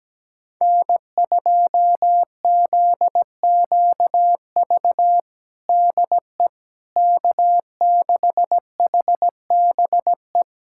Morse code, heard as a keyed tone, about 17 wpm, 700 Hz. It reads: N2ZQV DE K6HBE
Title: USA Callsign DE Callsign 17wpm